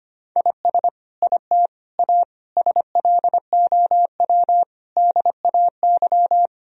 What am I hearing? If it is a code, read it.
Morse code, heard as a keyed tone, about 25 wpm, 700 Hz. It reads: IS IT A SLOW DAY
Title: Sentences from Top 1000 Words - Review Part 2 - 25wpm